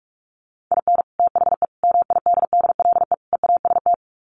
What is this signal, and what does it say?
Morse code, heard as a keyed tone, about 44 wpm, 700 Hz. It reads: IN THE MIDDLE EAST
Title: Sets of 4 Words 45wpm